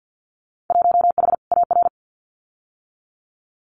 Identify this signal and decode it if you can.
Morse code, heard as a keyed tone, about 50 wpm, 700 Hz. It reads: 1H AR
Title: Winter Field Day Exchanges 50wpm